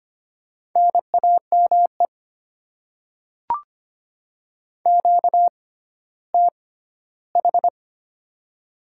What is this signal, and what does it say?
Morse code, heard as a keyed tone, about 25 wpm, spaced as for 14 wpm, 700 Hz. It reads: NAME QTH
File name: Top 20 QSO Elements - 6x Character Spacing - 25wpm